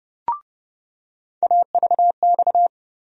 Morse code, heard as a keyed tone, about 31 wpm, 700 Hz. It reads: AVX
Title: USA Call Sign Postfixes - 3 Letters 30wpm